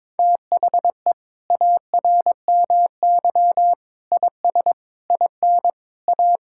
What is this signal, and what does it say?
Morse code, heard as a keyed tone, about 21 wpm, 700 Hz. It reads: THE ARMY IS IN A
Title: Sentences from Top 1000 Words - Review Part 6 - 22wpm